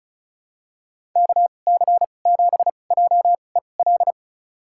Morse code, heard as a keyed tone, about 34 wpm, spaced as for 25 wpm, 700 Hz. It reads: KC7JEL
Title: USA Call Signs - 6 Characters - 2x Character Spacing - 35wpm